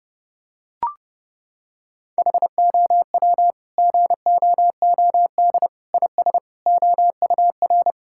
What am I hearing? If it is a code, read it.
Morse code, heard as a keyed tone, about 30 wpm, 700 Hz. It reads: HOW GOOD IS OUR
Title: Sentences from Top 1000 Words - Review Part 6 - 30wpm